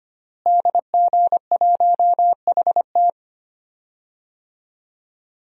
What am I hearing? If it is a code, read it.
Morse code, heard as a keyed tone, about 25 wpm, 700 Hz. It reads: DG1HT
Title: DXCC-100 Call Signs - 5 Characters - Repeated 3x - 25wpm